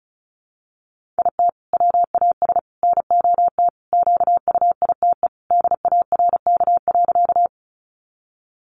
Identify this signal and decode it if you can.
Morse code, heard as a keyed tone, about 34 wpm, 700 Hz. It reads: IT WAS NOT QUITE DARK.
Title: Sentences from Top 700 Words 35wpm